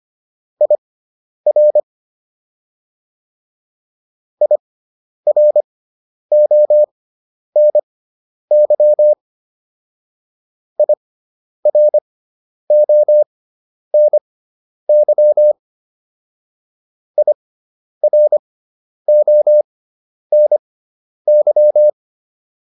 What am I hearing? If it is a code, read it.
Morse code, heard as a keyed tone, about 25 wpm, spaced as for 10 wpm, 600 Hz. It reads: IR IRONY IRONY IRONY